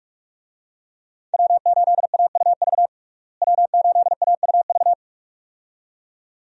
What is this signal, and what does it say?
Morse code, heard as a keyed tone, about 43 wpm, 700 Hz. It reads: W8AUV W8AUV
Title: USA Call Signs - 5 Characters - Repeated 2x - 45wpm